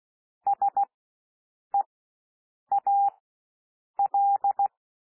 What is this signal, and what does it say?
Morse code, heard as a keyed tone, about 16 wpm, spaced as for 7 wpm, 800 Hz. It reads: SEAL
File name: cwmp3 / W1 SOUND WORDS 16 wpm